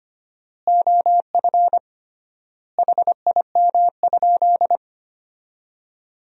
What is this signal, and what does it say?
Morse code, heard as a keyed tone, about 25 wpm, 700 Hz. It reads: OF HIM?
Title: 3x Word-Spacing - Sentences from Top 200 Words - 25wpm